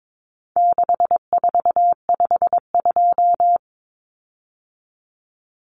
Morse code, heard as a keyed tone, about 22 wpm, 700 Hz. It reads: 6452
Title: Common Cut Numbers - 4-Digit 22wpm